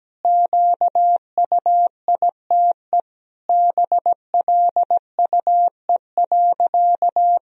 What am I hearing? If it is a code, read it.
Morse code, heard as a keyed tone, about 17 wpm, 700 Hz. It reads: QUITE BLUE.
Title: Sentences from Top 1000 Words - Review Part 3 - 17wpm